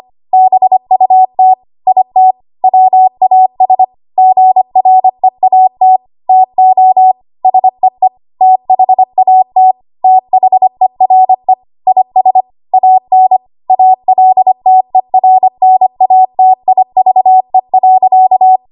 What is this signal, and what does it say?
Morse code, heard as a keyed tone, about 25 wpm, 750 Hz. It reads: BUT IT WAS GREAT TO SEE THAT THERE IS AN ALTERNATIVE.
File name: Archive / 25 WPM / 25WPM